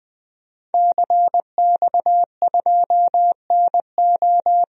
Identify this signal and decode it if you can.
Morse code, heard as a keyed tone, about 20 wpm, 700 Hz. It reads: CX2NO
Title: DXCC-100 Call Signs - 5 Characters - 20wpm